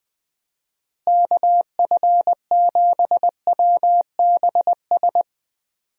Morse code, heard as a keyed tone, about 20 wpm, 700 Hz. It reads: KF7WBS
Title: USA Call Signs - 6 Characters - Repeated 3x - 20wpm